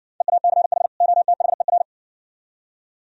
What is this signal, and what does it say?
Morse code, heard as a keyed tone, about 60 wpm, 700 Hz. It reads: EACH OTHER